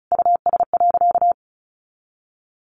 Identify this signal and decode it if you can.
Morse code, heard as a keyed tone, about 34 wpm, 700 Hz. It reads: US.
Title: Sentences from Top 400 Words 35wpm